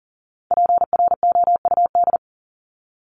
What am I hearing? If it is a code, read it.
Morse code, heard as a keyed tone, about 40 wpm, 700 Hz. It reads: PROUD
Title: Top 1501-1750 Words 40wpm